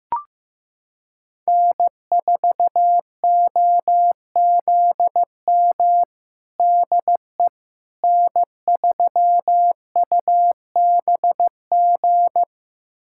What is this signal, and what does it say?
Morse code, heard as a keyed tone, about 15 wpm, 700 Hz. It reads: N4OZM DE N3UBG
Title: USA Callsign DE Callsign - 5 Characters - 15wpm